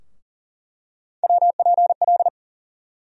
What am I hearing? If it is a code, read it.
Morse code, heard as a keyed tone, about 40 wpm, 700 Hz. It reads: WPL
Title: USA Call Sign Postfixes - 3 Letters 40wpm